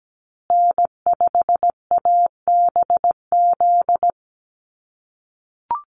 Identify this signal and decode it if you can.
Morse code, heard as a keyed tone, about 17 wpm, 700 Hz. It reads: N5ABZ
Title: USA Call Signs Repeated 2x 17wpm